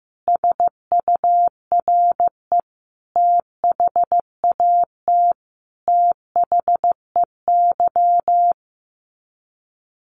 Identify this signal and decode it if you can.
Morse code, heard as a keyed tone, about 15 wpm, 700 Hz. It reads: SURE THAT THEY
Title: Sets of 5 Words Encore2 15wpm